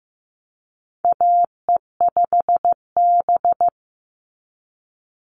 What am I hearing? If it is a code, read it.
Morse code, heard as a keyed tone, about 15 wpm, 700 Hz. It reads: AE5B
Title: NA SOTA Chaser Callsigns 15wpm